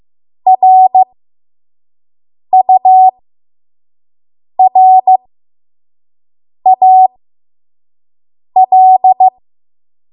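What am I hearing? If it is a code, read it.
Morse code, heard as a keyed tone, about 15 wpm, spaced as for 5 wpm, 750 Hz. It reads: RURAL
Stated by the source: 5 WPM Morse code practice.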